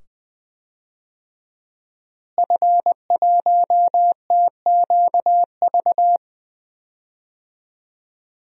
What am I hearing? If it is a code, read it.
Morse code, heard as a keyed tone, about 20 wpm, 700 Hz. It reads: F1TQV
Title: DXCC-100 Call Signs - 5 Characters - Repeated 2x - 20wpm